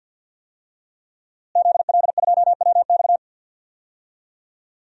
Morse code, heard as a keyed tone, about 51 wpm, 700 Hz. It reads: ZL2WX
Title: DXCC-100 Call Signs - 5 Characters - Repeated 2x - 50wpm